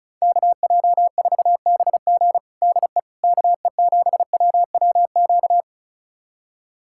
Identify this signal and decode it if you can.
Morse code, heard as a keyed tone, about 34 wpm, 700 Hz. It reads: KJ4BG DE KE7WWQ
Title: USA Callsign DE Callsign 35wpm